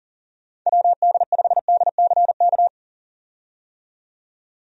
Morse code, heard as a keyed tone, about 39 wpm, 700 Hz. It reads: WD5DCK
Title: USA Call Signs - 6 Characters - 40wpm